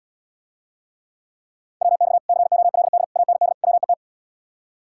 Morse code, heard as a keyed tone, about 80 wpm, 700 Hz. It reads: WX COLD AND WET